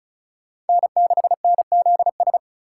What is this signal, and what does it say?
Morse code, heard as a keyed tone, about 34 wpm, 700 Hz. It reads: N6NZS
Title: USA Call Signs - 6x Character Spacing - 35wpm